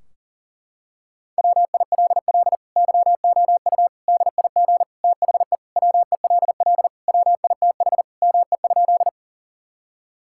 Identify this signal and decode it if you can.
Morse code, heard as a keyed tone, about 37 wpm, 700 Hz. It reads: WILL YOU DIG THE WELL WITH ME?
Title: Sentences from Top 1600 Words 40wpm